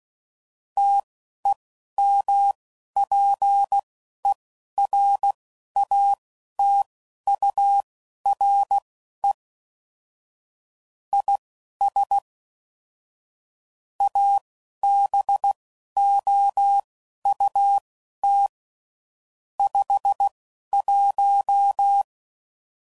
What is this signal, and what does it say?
Morse code, heard as a keyed tone, about 16 wpm, spaced as for 11 wpm, 800 Hz. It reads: TEMPERATURE IS ABOUT 51
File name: CW Code Groups @10WPM / Weather Sets / Weather 4 21